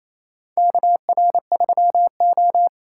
Morse code, heard as a keyed tone, about 29 wpm, 700 Hz. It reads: KR3O
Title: USA Call Signs - 4 Characters - 2x Character Spacing - 28wpm